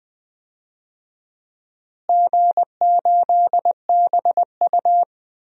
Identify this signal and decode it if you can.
Morse code, heard as a keyed tone, about 20 wpm, 700 Hz. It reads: G8BU
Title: DXCC-100 Call Signs - 4 Characters - Repeated 3x - 20wpm